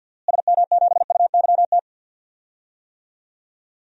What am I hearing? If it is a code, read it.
Morse code, heard as a keyed tone, about 46 wpm, 700 Hz. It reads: SM7UYT
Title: DXCC-100 Call Signs - 6 Characters - Repeated 2x - 50wpm